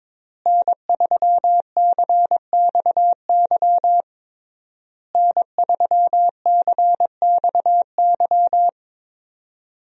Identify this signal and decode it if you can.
Morse code, heard as a keyed tone, about 22 wpm, 700 Hz. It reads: N3CXY N3CXY
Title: USA Call Signs - 5 Characters - Repeated 3x - 22wpm